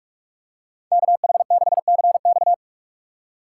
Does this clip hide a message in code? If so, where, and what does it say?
Yes: Morse code from 0.9 to 2.5 s, heard as a keyed tone, about 44 wpm, 700 Hz. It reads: KH6CX